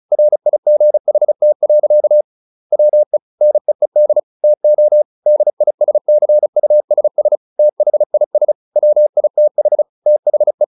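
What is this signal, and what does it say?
Morse code, heard as a keyed tone, about 34 wpm, 600 Hz. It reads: RIGHT. WE NEED TO DISCUSS THIS WITH THE
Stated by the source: Morse practice at 35 wpm from CW med Gnister, sending Ken Follett's novel Never in English.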